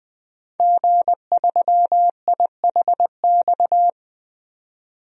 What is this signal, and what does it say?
Morse code, heard as a keyed tone, about 20 wpm, 700 Hz. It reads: G3IHX
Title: DXCC-100 Call Signs - 5 Characters - 20wpm